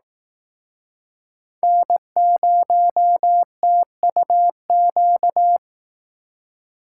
Morse code, heard as a keyed tone, about 19 wpm, 700 Hz. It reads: N0TUQ